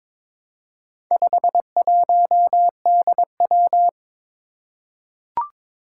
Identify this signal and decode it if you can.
Morse code, heard as a keyed tone, about 23 wpm, 700 Hz. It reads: 51DW
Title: Four Challenging Letter-Number Combinations 22wpm